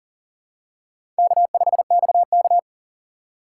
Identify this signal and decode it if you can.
Morse code, heard as a keyed tone, about 39 wpm, 700 Hz. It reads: K5XK
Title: K1USN SST QSO Practice - 40wpm